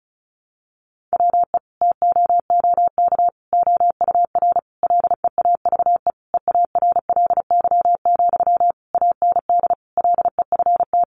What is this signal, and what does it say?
Morse code, heard as a keyed tone, about 34 wpm, 700 Hz. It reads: WE TOOK OUR LEAVE EARLY, AND LEFT
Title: Sentences from Top 400 Words 35wpm